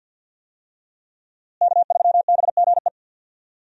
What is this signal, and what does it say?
Morse code, heard as a keyed tone, about 52 wpm, 700 Hz. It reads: K3BGE
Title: USA Call Signs - 5 Characters - Repeated 2x - 50wpm